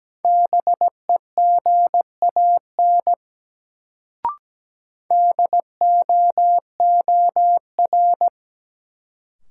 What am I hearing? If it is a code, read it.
Morse code, heard as a keyed tone, about 17 wpm, 700 Hz. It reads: BEGAN DOOR
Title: Sentences from Top 300 Words - New Word Drill - 17wpm